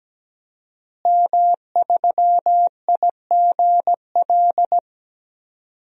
Morse code, heard as a keyed tone, about 17 wpm, 700 Hz. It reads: M3IGL
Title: DXCC-100 Call Signs - 5 Characters - Repeated 2x - 17wpm